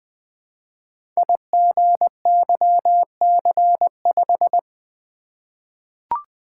Morse code, heard as a keyed tone, about 20 wpm, 700 Hz. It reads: IGYC5
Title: Code Groups - 5 Letter:Numbers - 20wpm